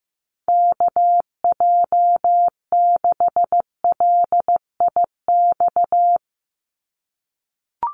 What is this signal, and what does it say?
Morse code, heard as a keyed tone, about 15 wpm, 700 Hz. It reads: KJ6LIX
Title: USA Call Signs 15wpm